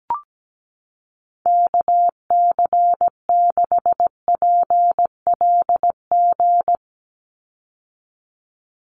Morse code, heard as a keyed tone, about 17 wpm, 700 Hz. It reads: KC6PLG
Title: USA Call Signs 17wpm